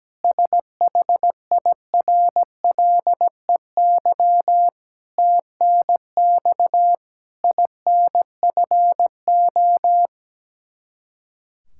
Morse code, heard as a keyed tone, about 17 wpm, 700 Hz. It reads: SHIRLEY TNX INFO
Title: K1USN SST QSO Practice 17wpm